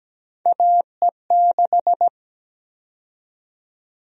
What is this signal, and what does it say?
Morse code, heard as a keyed tone, about 17 wpm, 700 Hz. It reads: AE6
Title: USA Call Sign Prefixes - Repeated 2x - 17wpm